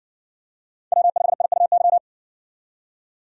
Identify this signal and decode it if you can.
Morse code, heard as a keyed tone, about 60 wpm, 700 Hz. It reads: W5IUY